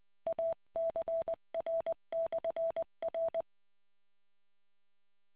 Morse code, heard as a keyed tone, about 20 wpm, 660 Hz. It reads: ACR/R